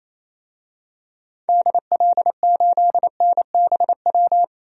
Morse code, heard as a keyed tone, about 28 wpm, 700 Hz. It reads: DL8NBW
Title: DXCC-100 Call Signs - 6 Characters - 28wpm